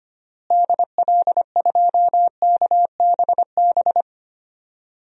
Morse code, heard as a keyed tone, about 25 wpm, 700 Hz. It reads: DL2KBB
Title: DXCC-100 Call Signs - 6 Characters - Repeated 3x - 25wpm